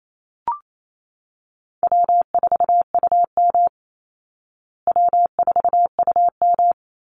Morse code, heard as a keyed tone, about 28 wpm, 700 Hz. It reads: W4UM W4UM
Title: Easy Call Signs Repeated 2x - 28wpm